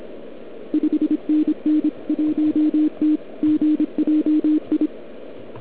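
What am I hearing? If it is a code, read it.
Morse code, heard as a keyed tone, about 26 wpm, 300 Hz. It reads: 5NN 1T GJI
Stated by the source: Morse from the OK QRP 2013 contest, run at 1 watt on a 12.3V lithium battery, Liberci, with OK1MKX also at 1W.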